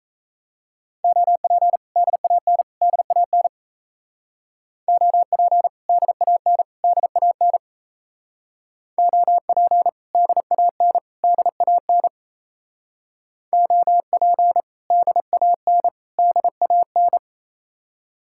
Morse code, heard as a keyed tone, about 36 wpm, 700 Hz. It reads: OP DAN DAN OP DAN DAN OP DAN DAN OP DAN DAN